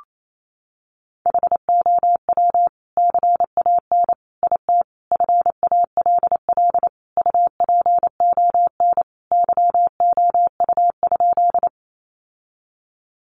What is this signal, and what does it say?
Morse code, heard as a keyed tone, about 28 wpm, 700 Hz. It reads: HOW CAN IT FALL UPON YOU?
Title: Sentences from Top 1100 Words 28wpm